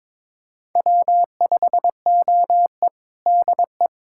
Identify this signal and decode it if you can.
Morse code, heard as a keyed tone, about 22 wpm, 700 Hz. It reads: W5OE DE
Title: USA Callsign DE Callsign - 4 Characters - 22wpm